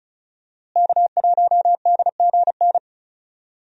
Morse code, heard as a keyed tone, about 34 wpm, 700 Hz. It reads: K1DGN
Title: USA Call Signs - 5 Characters - Repeated 2x - 35wpm